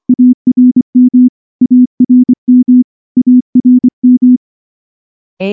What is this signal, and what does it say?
Morse code, heard as a keyed tone, about 23 wpm, 250 Hz. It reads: ARM ARM ARM
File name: morse / morse-training-25WPM